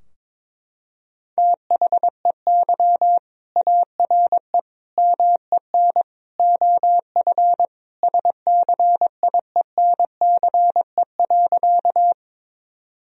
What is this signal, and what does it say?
Morse code, heard as a keyed tone, about 22 wpm, 700 Hz. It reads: THEY ARE MEN OF SCIENCE.
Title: Sentences from Top 1000 Words - Review Part 2 - 22wpm